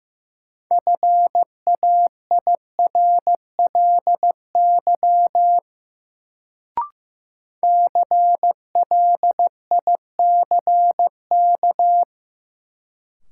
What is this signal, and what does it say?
Morse code, heard as a keyed tone, about 16 wpm, 700 Hz. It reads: FAIRLY CLICK
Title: Sentences from Top 1100 Words - New Word Drill Alt - 15wpm